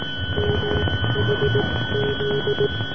Morse code, heard as a keyed tone, about 18 wpm, spaced as for 14 wpm, 420 Hz, with noise. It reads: MHZ